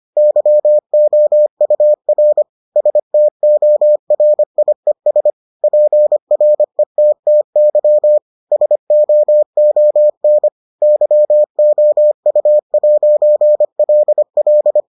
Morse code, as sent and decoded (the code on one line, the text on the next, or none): -.-- --- ..- .-. / ... - --- .-. .. ... / .--. .-. - - -.-- / ... --- --- -. / -.-- --- ..- .----. .-.. .-..
YOUR STORIES PRETTY SOON YOU'LL